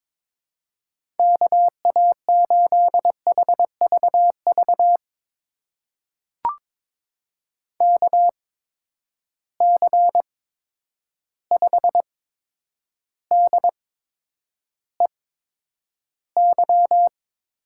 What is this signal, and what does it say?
Morse code, heard as a keyed tone, about 22 wpm, spaced as for 9 wpm, 700 Hz. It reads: KA8HVV KC5DEY
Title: USA Call Signs - 8x Character Spacing - 22wpm